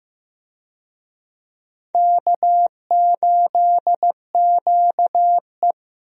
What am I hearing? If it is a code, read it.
Morse code, heard as a keyed tone, about 16 wpm, 700 Hz. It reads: K8QE
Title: USA Call Signs - 4 Characters - Repeated 2x - 15wpm